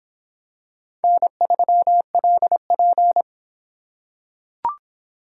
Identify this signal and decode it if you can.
Morse code, heard as a keyed tone, about 26 wpm, 700 Hz. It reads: N3LP